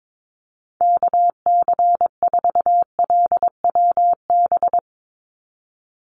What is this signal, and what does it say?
Morse code, heard as a keyed tone, about 22 wpm, 700 Hz. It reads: KC4LWB
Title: USA Call Signs 22wpm